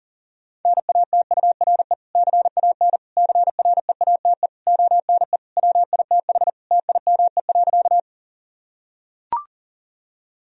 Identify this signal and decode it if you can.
Morse code, heard as a keyed tone, about 40 wpm, 700 Hz. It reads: NATURE CAN CREATE ONE WITH TIME.
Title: Sentences from Top 1000 Words - Review Part 3 - 40wpm